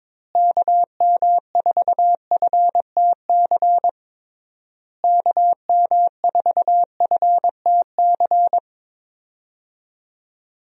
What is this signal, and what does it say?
Morse code, heard as a keyed tone, about 22 wpm, 700 Hz. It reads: KM4FTC KM4FTC
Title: USA Call Signs - 6 Characters - Repeated 3x - 22wpm